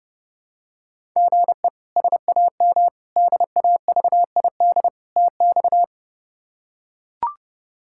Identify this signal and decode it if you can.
Morse code, heard as a keyed tone, about 32 wpm, 700 Hz. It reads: GE SAM DAVID TX
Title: K1USN SST QSO Practice - 30wpm